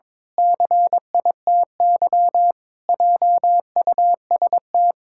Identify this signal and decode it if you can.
Morse code, heard as a keyed tone, about 22 wpm, 700 Hz. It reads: CITY JUST
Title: Pangram Challenge v2 - 22wpm